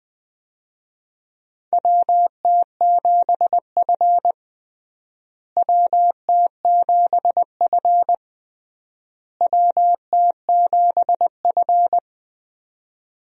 Morse code, heard as a keyed tone, about 21 wpm, 700 Hz. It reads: WT7F WT7F WT7F
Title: USA Call Signs - 4 Characters - Repeated 3x - 20wpm